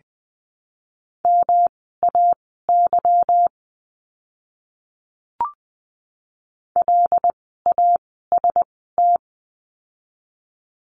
Morse code, heard as a keyed tone, about 20 wpm, spaced as for 14 wpm, 700 Hz. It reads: MAY LAST
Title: 244 Lesson 39f - Letter z - Words 20wpm